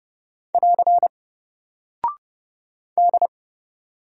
Morse code, heard as a keyed tone, about 30 wpm, 700 Hz. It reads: <AR> D
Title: Single Character (Letter:Number:Punctuation:Prosign) 30wpm